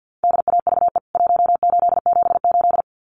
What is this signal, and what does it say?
Morse code, heard as a keyed tone, about 50 wpm, 700 Hz. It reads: DAVE 1878
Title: CWT Exchange 50wpm